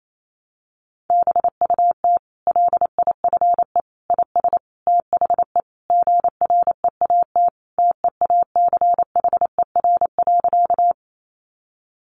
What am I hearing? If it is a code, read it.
Morse code, heard as a keyed tone, about 28 wpm, 690 Hz. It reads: BUT LIFE IS THE GREAT TEACHER.
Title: Sentences from Top 900 Words 28wpm